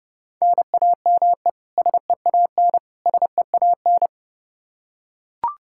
Morse code, heard as a keyed tone, about 32 wpm, 700 Hz. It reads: NAME SEAN SEAN
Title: Practice QSO - Exchange 1 - Beginning - 30wpm